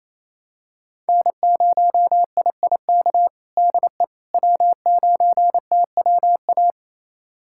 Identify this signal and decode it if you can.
Morse code, heard as a keyed tone, about 27 wpm, 700 Hz. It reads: N0IIK DE W9TWA
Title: USA Callsign DE Callsign - 5 Characters - 28wpm